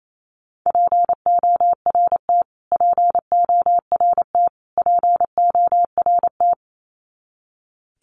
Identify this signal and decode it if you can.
Morse code, heard as a keyed tone, about 28 wpm, 700 Hz. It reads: PORT PORT PORT
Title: Top 200 Words Repeated 28wpm